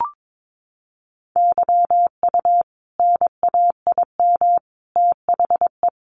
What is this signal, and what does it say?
Morse code, heard as a keyed tone, about 22 wpm, 700 Hz. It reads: YU NAIM THE
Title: A Sound Of Thunder 22wpm